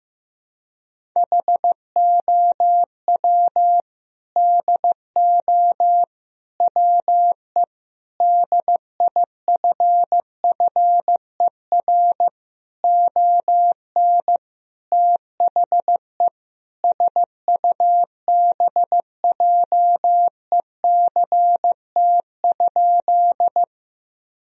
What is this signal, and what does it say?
Morse code, heard as a keyed tone, about 16 wpm, 700 Hz. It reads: HOW DO WE DIFFER ON THE SUBJECT?
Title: Sentences from Top 1000 Words - Review Part 1 - 15wpm